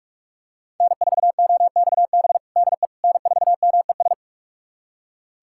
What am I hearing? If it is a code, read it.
Morse code, heard as a keyed tone, about 46 wpm, 700 Hz. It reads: N4OXB DE N4MES
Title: USA Callsign DE Callsign - 5 Characters - 45wpm